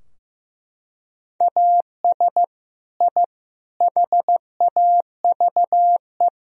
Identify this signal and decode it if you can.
Morse code, heard as a keyed tone, about 15 wpm, 700 Hz. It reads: AS I HAVE
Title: Sentences from Top 1000 Words - Review Part 3 - 15wpm